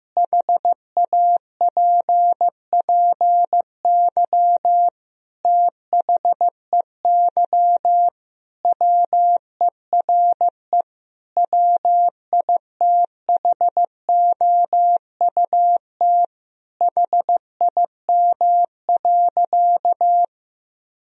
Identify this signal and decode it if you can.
Morse code, heard as a keyed tone, about 15 wpm, 700 Hz. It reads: HAPPY THEY WERE WITHOUT HIM.
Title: Sentences from Top 1000 Words - Review Part 3 - 15wpm